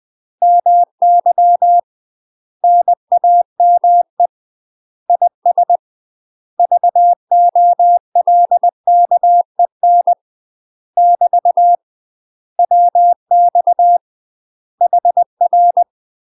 Morse code, heard as a keyed tone, about 20 wpm, 700 Hz. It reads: MY NAME IS VOLKEN <BT> WX HR
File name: UK MASTER QSOs TRAINING / 20WPM / UK MASTER QSOs 0025